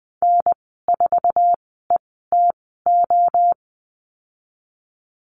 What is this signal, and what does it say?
Morse code, heard as a keyed tone, about 19 wpm, spaced as for 14 wpm, 700 Hz. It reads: N4ETO
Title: 088 Lesson 15g - Number 5 - Call Signs 20wpm